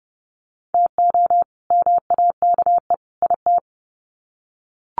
Morse code, heard as a keyed tone, about 30 wpm, 700 Hz. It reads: TO MAKE IT
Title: Sets of 3 Words 30wpm